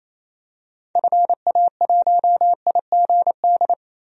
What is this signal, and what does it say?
Morse code, heard as a keyed tone, about 28 wpm, 700 Hz. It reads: FA1IGD
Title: DXCC-100 Call Signs - 6 Characters - 28wpm